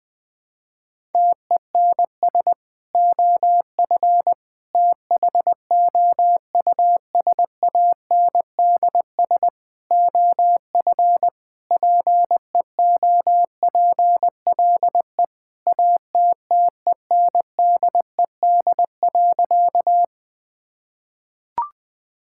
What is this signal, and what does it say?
Morse code, heard as a keyed tone, about 21 wpm, 700 Hz. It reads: TENS OF THOUSANDS OF PEOPLE ATTENDED.
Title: Sentences with 4-Word Phrases - Spoken Beginning - 20wpm